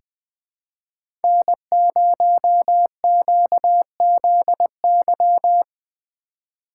Morse code, heard as a keyed tone, about 20 wpm, 700 Hz. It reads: N0QZY
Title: USA Call Signs - 5 Characters - 3x Character Spacing - 20wpm